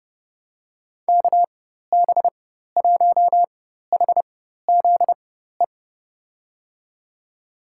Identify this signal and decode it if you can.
Morse code, heard as a keyed tone, about 32 wpm, spaced as for 14 wpm, 700 Hz. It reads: KB1HZE
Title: USA Call Signs - 6 Characters - 4x Character Spacing - 30wpm